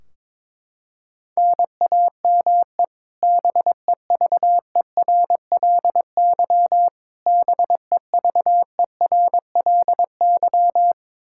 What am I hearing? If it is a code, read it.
Morse code, heard as a keyed tone, about 22 wpm, 700 Hz. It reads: NAME BEVERLY BEVERLY
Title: Practice QSO - Exchange 1 - Beginning - 22wpm